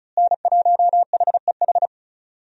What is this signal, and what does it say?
Morse code, heard as a keyed tone, about 36 wpm, 700 Hz. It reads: N1HEH
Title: USA Call Signs - 5 Characters - Repeated 2x - 35wpm